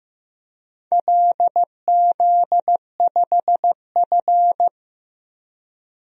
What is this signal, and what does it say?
Morse code, heard as a keyed tone, about 15 wpm, 700 Hz. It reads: LZ5F
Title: DXCC-100 Call Signs - 4 Characters - Repeated 3x - 15wpm